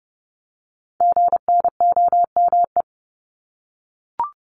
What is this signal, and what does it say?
Morse code, heard as a keyed tone, about 30 wpm, 700 Hz. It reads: GNOME
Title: Multi-letter Phonemes Repeated 3x 30wpm